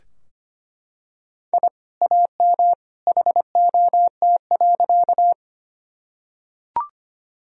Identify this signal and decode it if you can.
Morse code, heard as a keyed tone, about 25 wpm, 700 Hz. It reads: I AM HOT.
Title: Sentences from Top 1000 Words - Review Part 3 - 25wpm